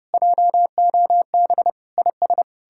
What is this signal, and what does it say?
Morse code, heard as a keyed tone, about 40 wpm, 700 Hz. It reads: JOB IS